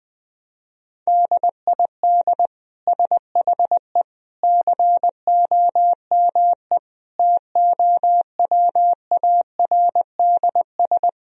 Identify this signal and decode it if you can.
Morse code, heard as a keyed tone, about 21 wpm, 700 Hz. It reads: DID SHE COME TOWARDS
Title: Sentences from Top 1000 Words - Review Part 2 - 20wpm